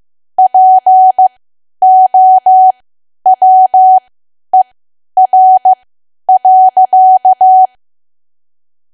Morse code, heard as a keyed tone, about 15 wpm, spaced as for 10 wpm, 750 Hz. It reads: POWER.
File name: cwa / W1AW-Level2 / 10WPM